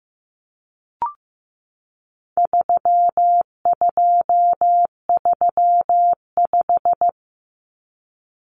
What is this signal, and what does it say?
Morse code, heard as a keyed tone, about 15 wpm, 700 Hz. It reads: 3235
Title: Numbers - 4-Digit 15wpm